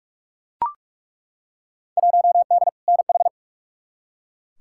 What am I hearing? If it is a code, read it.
Morse code, heard as a keyed tone, about 43 wpm, 700 Hz. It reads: 1D NH
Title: Field Day Exchanges - 45wpm